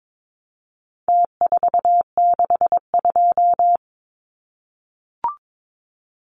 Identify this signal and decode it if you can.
Morse code, heard as a keyed tone, about 22 wpm, 700 Hz. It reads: T462
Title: Numbers - Contest Style - 4-Digit 22wpm